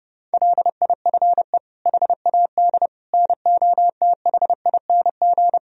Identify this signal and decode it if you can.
Morse code, heard as a keyed tone, about 30 wpm, 700 Hz. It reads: LIFE HAD NOTHING
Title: Sentences from Top 1000 Words - Review Part 4 - 30wpm